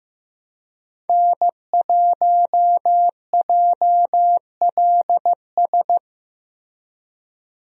Morse code, heard as a keyed tone, about 15 wpm, 700 Hz. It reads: N1JLS
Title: USA Call Signs - 5 Characters - Repeated 2x - 15wpm